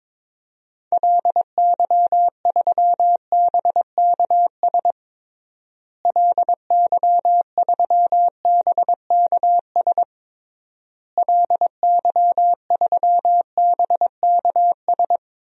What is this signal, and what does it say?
Morse code, heard as a keyed tone, about 22 wpm, 700 Hz. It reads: LY3BKS LY3BKS LY3BKS
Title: DXCC-100 Call Signs - 6 Characters - Repeated 3x - 22wpm